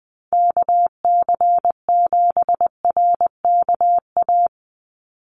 Morse code, heard as a keyed tone, about 20 wpm, 700 Hz. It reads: KC7RKA
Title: USA Call Signs 20wpm